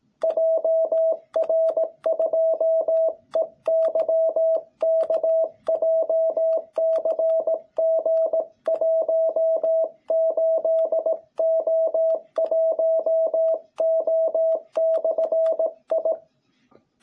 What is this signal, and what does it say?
Morse code, heard as a keyed tone, about 17 wpm, 660 Hz. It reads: JR2EYK JCG18O1O/I
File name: jh8cbh / 2021cw